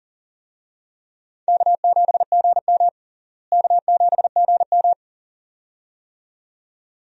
Morse code, heard as a keyed tone, about 40 wpm, 700 Hz. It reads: K7GM K7GM
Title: USA Call Signs - 4 Characters - Repeated 2x - 40wpm